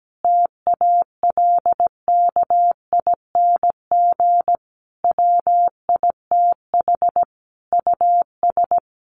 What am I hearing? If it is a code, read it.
Morse code, heard as a keyed tone, about 17 wpm, 700 Hz. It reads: TALKING WITH US
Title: Sets of 5 Words Encore 17wpm